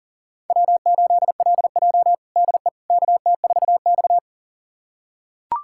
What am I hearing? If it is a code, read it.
Morse code, heard as a keyed tone, about 40 wpm, 700 Hz. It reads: W8LJ DE KT4X
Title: USA Callsign DE Callsign - 4 Characters - 40wpm